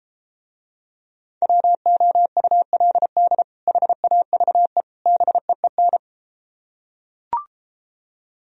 Morse code, heard as a keyed tone, about 33 wpm, 700 Hz. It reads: WOULD HAVE BEEN